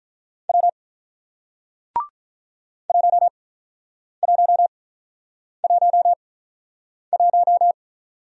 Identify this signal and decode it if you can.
Morse code, heard as a keyed tone, about 46 wpm, spaced as for 9 wpm, 700 Hz. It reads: W 1111